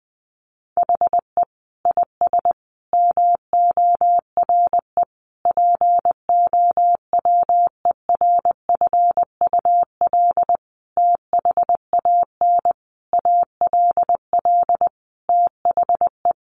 Morse code, heard as a keyed tone, about 20 wpm, 700 Hz. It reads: HE IS MORE POWERFUL THAN ALL THE
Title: Sentences from Top 1300 Words 20wpm